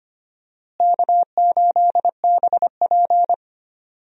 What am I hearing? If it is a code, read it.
Morse code, heard as a keyed tone, about 25 wpm, 700 Hz. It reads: K8BP
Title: USA Call Signs - 4 Characters - Repeated 3x - 25wpm